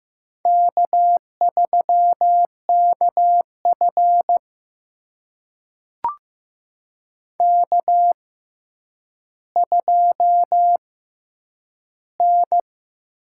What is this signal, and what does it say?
Morse code, heard as a keyed tone, about 15 wpm, spaced as for 9 wpm, 700 Hz. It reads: K3KF K2N
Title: USA Call Signs - 4 Characters - 6x Character Spacing - 15wpm